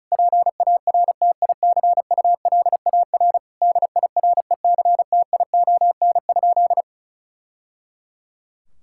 Morse code, heard as a keyed tone, about 36 wpm, 700 Hz. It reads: PARTICULAR DIRECTION?
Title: Sentences from Top 1000 Words - Review Part 5 - 35wpm